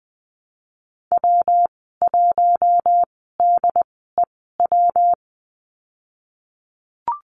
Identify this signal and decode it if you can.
Morse code, heard as a keyed tone, about 20 wpm, spaced as for 15 wpm, 700 Hz. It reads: W1DEW
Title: 173 Lesson 28g - Number 9 - Call Signs 20wpm